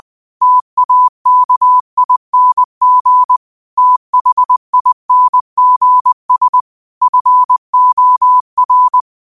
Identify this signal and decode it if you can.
Morse code, heard as a keyed tone, about 20 wpm, 1000 Hz. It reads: TAKING THINGS FOR
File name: qotdfiles / 20wpm